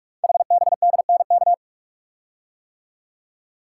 Morse code, heard as a keyed tone, about 44 wpm, 690 Hz. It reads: HBDNK
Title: Code Groups - 5 Letters - 45wpm